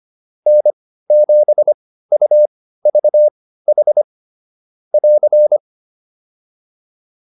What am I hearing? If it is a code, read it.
Morse code, heard as a keyed tone, about 25 wpm, spaced as for 15 wpm, 600 Hz. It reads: N7UVH <AR>